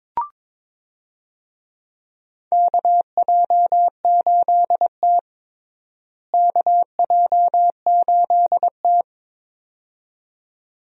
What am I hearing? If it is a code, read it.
Morse code, heard as a keyed tone, about 22 wpm, 700 Hz. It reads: KJ8T KJ8T
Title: USA Call Signs - 4 Characters - Repeated 2x - 22wpm